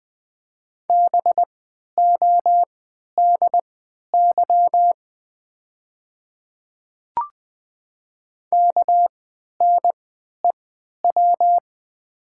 Morse code, heard as a keyed tone, about 20 wpm, spaced as for 11 wpm, 700 Hz. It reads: BODY KNEW
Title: 3x Character Spacing - Top 500 Words - 20wpm